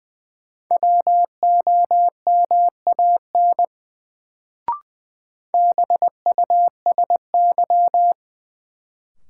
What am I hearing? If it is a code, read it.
Morse code, heard as a keyed tone, about 21 wpm, 700 Hz. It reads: WOMAN BUSY
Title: Sentences from Top 1000 Words - Review Part 2 - New Word Drill - 20wpm